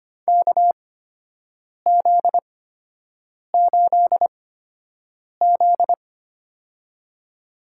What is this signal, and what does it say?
Morse code, heard as a keyed tone, about 25 wpm, spaced as for 7 wpm, 700 Hz. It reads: KZ8Z
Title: USA Call Signs - 4 Characters - 8x Character Spacing - 25wpm